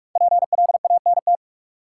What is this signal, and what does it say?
Morse code, heard as a keyed tone, about 46 wpm, 700 Hz. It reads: PLANT
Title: Sentences from Top 1000 Words - Review Part 1 - New Word Drill - 45wpm